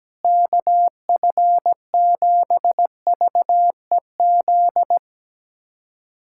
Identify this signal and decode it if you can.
Morse code, heard as a keyed tone, about 17 wpm, 700 Hz. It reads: KF7VEZ
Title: USA Callsign DE Callsign 17wpm